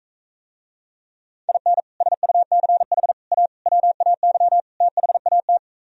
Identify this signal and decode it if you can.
Morse code, heard as a keyed tone, about 45 wpm, 700 Hz. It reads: IN SUCH A WAY THAT